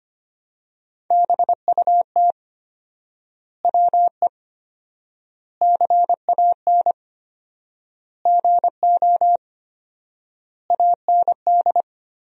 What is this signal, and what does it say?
Morse code, heard as a keyed tone, about 25 wpm, 700 Hz. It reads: BUT WE CAN GO AND
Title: 4x Word-Spacing - Sentences from Top 100 Words - 25wpm